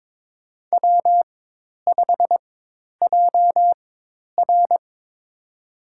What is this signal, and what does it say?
Morse code, heard as a keyed tone, about 21 wpm, spaced as for 10 wpm, 700 Hz. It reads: W5JR
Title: USA Call Signs - 4 Characters - 4x Character Spacing - 22wpm